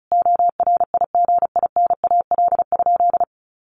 Morse code, heard as a keyed tone, about 34 wpm, 700 Hz. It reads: ORIGINAL?
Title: Sentences from Top 1500 Words 35wpm